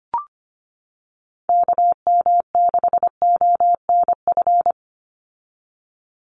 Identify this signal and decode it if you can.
Morse code, heard as a keyed tone, about 25 wpm, 700 Hz. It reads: KM6ONF
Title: USA Call Signs 25wpm